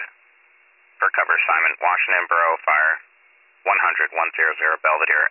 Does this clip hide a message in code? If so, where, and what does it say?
Yes: Emergency Alert System (SAME) tones from 1.0 to 4.7 s, heard as harsh buzzing bursts, with speech.